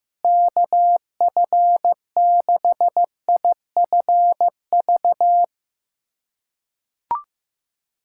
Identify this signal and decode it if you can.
Morse code, heard as a keyed tone, about 15 wpm, 700 Hz. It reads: KF6IFV
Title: USA Callsign DE Callsign - 6 Characters - 15wpm